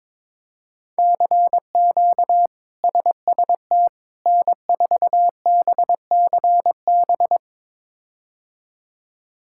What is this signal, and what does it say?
Morse code, heard as a keyed tone, about 22 wpm, 700 Hz. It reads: CQ SST N4BCB
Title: K1USN SST QSO Practice - 22wpm